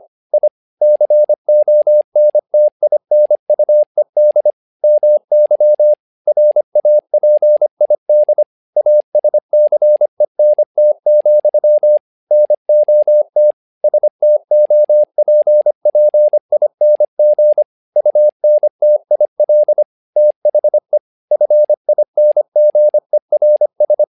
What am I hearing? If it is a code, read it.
Morse code, heard as a keyed tone, about 25 wpm, 600 Hz. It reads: I CONTINUED MY RAPID ASCENT, NOT STOPPING UNTIL THE FINGERS
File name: learning center / tower trouble / tower trouble 25wpm